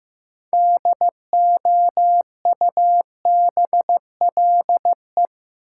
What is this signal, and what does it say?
Morse code, heard as a keyed tone, about 15 wpm, 700 Hz. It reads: DOUBLE
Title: Sentences from Top 1400 Words - New Word Drill - 15wpm